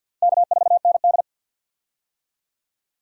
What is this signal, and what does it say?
Morse code, heard as a keyed tone, about 46 wpm, 700 Hz. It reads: K4ND
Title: USA Call Signs - 4 Characters - 50wpm